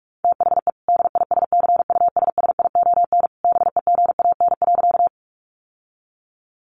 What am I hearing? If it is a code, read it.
Morse code, heard as a keyed tone, about 44 wpm, 700 Hz. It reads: THE DISCUSSION BEGAN.
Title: Sentences from Top 1100 Words 45wpm